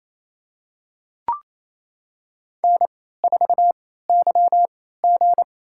Morse code, heard as a keyed tone, about 28 wpm, spaced as for 16 wpm, 700 Hz. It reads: N4YG
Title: USA Call Signs - 4 Characters - 3x Character Spacing - 28wpm